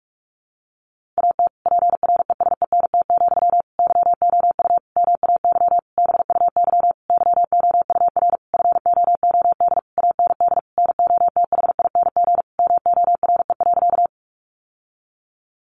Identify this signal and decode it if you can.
Morse code, heard as a keyed tone, about 44 wpm, 700 Hz. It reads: AT PRESENT, YOU MAY BUY YOUR FOOD AND NOTHING MORE.
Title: Sentences from Top 800 Words 45wpm